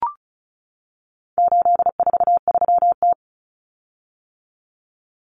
Morse code, heard as a keyed tone, about 34 wpm, 700 Hz. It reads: 843T
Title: Common Cut Numbers - 4-Digit 35wpm